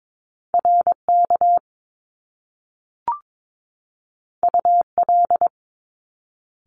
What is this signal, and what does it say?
Morse code, heard as a keyed tone, about 22 wpm, 700 Hz. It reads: RK UL
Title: Top Two Letter Combinations in Words 22wpm